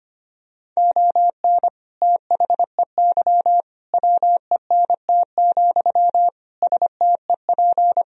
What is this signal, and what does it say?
Morse code, heard as a keyed tone, about 25 wpm, 700 Hz. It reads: ON THEY WENT, STEP
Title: Sentences from Top 1000 Words - Review Part 2 - 25wpm